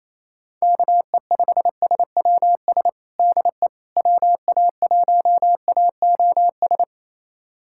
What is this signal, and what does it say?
Morse code, heard as a keyed tone, about 28 wpm, 700 Hz. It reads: KE5SWS DE WA1AOS
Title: USA Callsign DE Callsign - 6 Characters - 28wpm